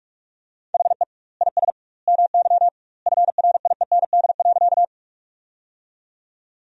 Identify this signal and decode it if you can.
Morse code, heard as a keyed tone, about 46 wpm, 700 Hz. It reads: HE IS MY FRIEND.
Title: 2x Word-Spacing - Sentences from Top 300 Words - 45wpm